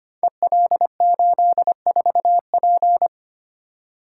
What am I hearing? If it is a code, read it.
Morse code, heard as a keyed tone, about 25 wpm, 700 Hz. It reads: EL84P
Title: Code Groups - 5 Letter:Numbers - 25wpm